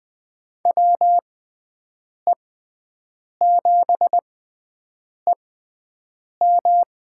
Morse code, heard as a keyed tone, about 20 wpm, spaced as for 7 wpm, 700 Hz. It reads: WE7EM
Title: USA Call Signs - 5 Characters - 6x Character Spacing - 20wpm